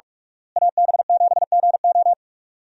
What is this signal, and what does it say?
Morse code, heard as a keyed tone, about 47 wpm, 700 Hz. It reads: AB7GO
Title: USA Call Signs - 5 Characters - Repeated 2x - 45wpm